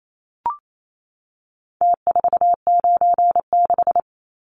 Morse code, heard as a keyed tone, about 28 wpm, 700 Hz. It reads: T496
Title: Numbers - Contest Style - 4-Digit 28wpm